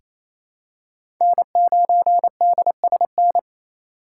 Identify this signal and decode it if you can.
Morse code, heard as a keyed tone, about 28 wpm, 700 Hz. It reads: N9DSN
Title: USA Call Signs - 5 Characters - 2x Character Spacing - 28wpm